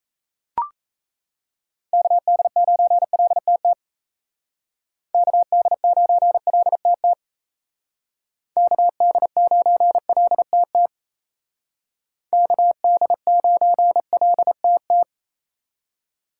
Morse code, heard as a keyed tone, about 37 wpm, 700 Hz. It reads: KD9LTT KD9LTT KD9LTT KD9LTT